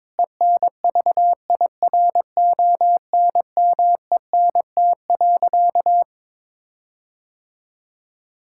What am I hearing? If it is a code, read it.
Morse code, heard as a keyed tone, about 22 wpm, 700 Hz. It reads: ENVIRONMENT.
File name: Difficult Sentences with Hints - 22wpm